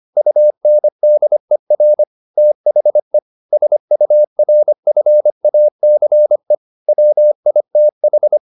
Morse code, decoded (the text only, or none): UNDER THE SURFACE WITH